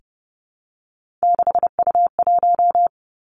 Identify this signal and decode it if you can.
Morse code, heard as a keyed tone, about 32 wpm, 700 Hz. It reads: 6U1
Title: Three Challenging Letter-Number Combinations 30wpm